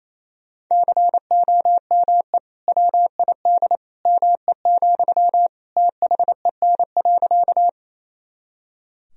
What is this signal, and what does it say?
Morse code, heard as a keyed tone, about 28 wpm, 700 Hz. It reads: COME WID ME, THEN.
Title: Sentences from Top 200 Words and Abbreviations 28wpm